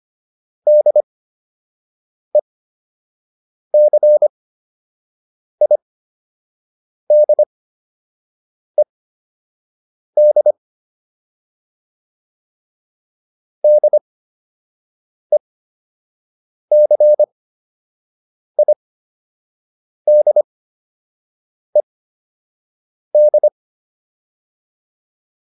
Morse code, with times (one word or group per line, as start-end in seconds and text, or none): 0.7-10.5 DECIDED
13.6-23.5 DECIDED